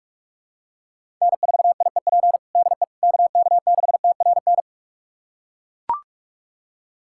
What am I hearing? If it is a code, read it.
Morse code, heard as a keyed tone, about 41 wpm, 700 Hz. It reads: N4IEP DE KK6TRN
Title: USA Callsign DE Callsign 45wpm